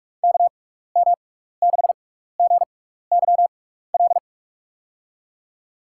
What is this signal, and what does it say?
Morse code, heard as a keyed tone, about 47 wpm, spaced as for 16 wpm, 690 Hz. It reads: KM6GYL
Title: USA Call Signs - 6x Character Spacing - 45wpm